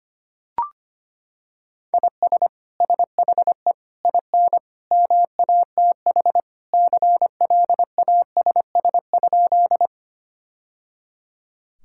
Morse code, heard as a keyed tone, about 25 wpm, 700 Hz. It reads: IS SHE IN MATH CLASS?
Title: Sentences from Top 1000 Words - Review Part 6 - 25wpm